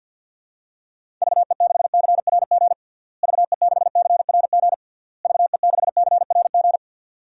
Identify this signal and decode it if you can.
Morse code, heard as a keyed tone, about 51 wpm, 700 Hz. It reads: VE6CRG VE6CRG VE6CRG
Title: DXCC-100 Call Signs - 6 Characters - Repeated 3x - 50wpm